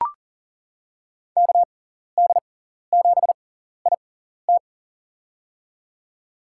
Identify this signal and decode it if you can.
Morse code, heard as a keyed tone, about 40 wpm, spaced as for 14 wpm, 700 Hz. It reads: KD7IT
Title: USA Call Signs - 6x Character Spacing - 40wpm